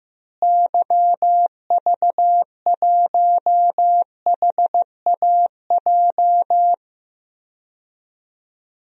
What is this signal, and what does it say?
Morse code, heard as a keyed tone, about 16 wpm, 700 Hz. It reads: YV1HAJ
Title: DXCC-100 Call Signs - 6 Characters - Repeated 2x - 15wpm